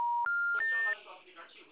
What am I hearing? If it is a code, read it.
Special information tone, with speech.
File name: out- -2011- - - .52842